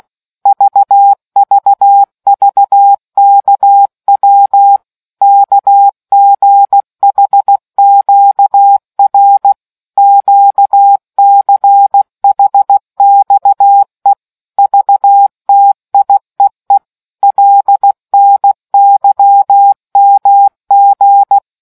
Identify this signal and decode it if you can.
Morse code, heard as a keyed tone, about 16 wpm, 800 Hz. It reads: VVVKW KGHQR QCHXE VTIEE LNYMG